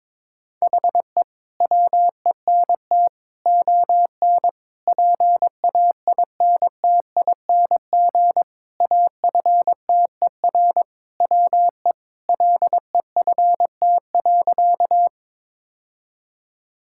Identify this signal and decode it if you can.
Morse code, heard as a keyed tone, about 22 wpm, 700 Hz. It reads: HE WENT ON PAINTING AFTER WE LEFT.
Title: Sentences from Top 1000 Words - Review Part 6 - 22wpm